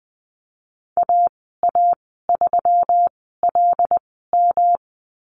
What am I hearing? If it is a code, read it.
Morse code, heard as a keyed tone, about 20 wpm, spaced as for 15 wpm, 700 Hz. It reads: AA3LM
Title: 186 Lesson 30g - Letter b - Call Signs 20wpm